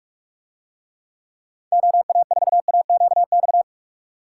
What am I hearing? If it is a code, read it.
Morse code, heard as a keyed tone, about 42 wpm, 700 Hz. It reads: OA4AQX